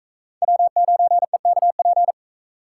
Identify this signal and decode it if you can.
Morse code, heard as a keyed tone, about 38 wpm, 700 Hz. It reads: W9EKP